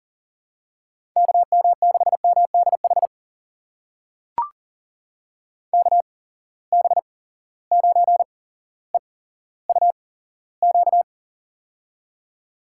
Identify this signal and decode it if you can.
Morse code, heard as a keyed tone, about 40 wpm, spaced as for 17 wpm, 700 Hz. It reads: KM6MDH KB9EUQ
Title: USA Call Signs - 8x Character Spacing - 40wpm